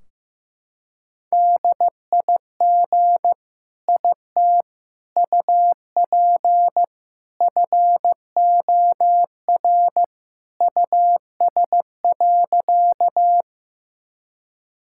Morse code, heard as a keyed tone, about 16 wpm, 700 Hz. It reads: DIG IT UP FOR US.
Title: Sentences from Top 1600 Words 15wpm